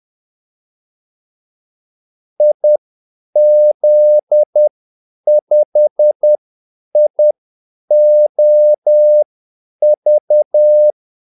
Morse code, heard as a keyed tone, about 10 wpm, spaced as for 8 wpm, 600 Hz. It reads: IZ5IOV
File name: Callsigns-10-8wpm